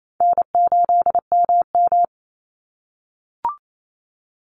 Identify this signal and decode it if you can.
Morse code, heard as a keyed tone, about 28 wpm, 700 Hz. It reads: N8MM
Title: USA Call Signs 28wpm